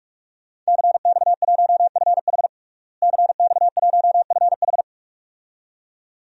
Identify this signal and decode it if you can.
Morse code, heard as a keyed tone, about 42 wpm, 700 Hz. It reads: CX1FH CX1FH
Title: DXCC-100 Call Signs - 5 Characters - Repeated 2x - 45wpm